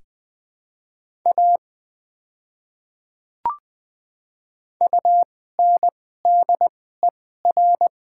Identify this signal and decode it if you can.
Morse code, heard as a keyed tone, about 20 wpm, spaced as for 14 wpm, 700 Hz. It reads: A UNDER
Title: 2x Character Spacing - Top 200 Words - 20wpm